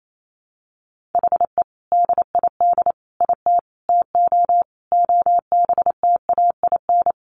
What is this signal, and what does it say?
Morse code, heard as a keyed tone, about 28 wpm, 700 Hz. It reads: HE DID IT TO OBTAIN
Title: Sentences from Top 1500 Words 28wpm